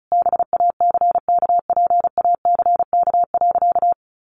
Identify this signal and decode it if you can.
Morse code, heard as a keyed tone, about 34 wpm, 700 Hz. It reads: BACKPACK.
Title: Additional Easy Phrases 35wpm